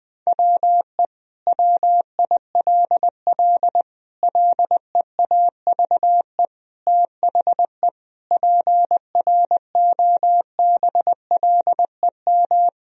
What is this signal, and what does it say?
Morse code, heard as a keyed tone, about 20 wpm, 700 Hz. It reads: WE WILL LEAVE THE PROBLEM
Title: Sentences from Top 1000 Words - Review Part 2 - 20wpm